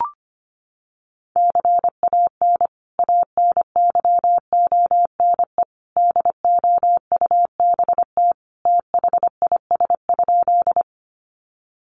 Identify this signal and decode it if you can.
Morse code, heard as a keyed tone, about 25 wpm, 700 Hz. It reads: CAN ANYONE DOUBT THIS?
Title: Sentences from Top 1500 Words 25wpm